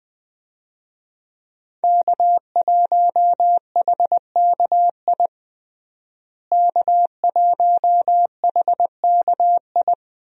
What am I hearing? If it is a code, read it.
Morse code, heard as a keyed tone, about 20 wpm, 700 Hz. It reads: K1HKI K1HKI
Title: USA Call Signs - 5 Characters - Repeated 2x - 20wpm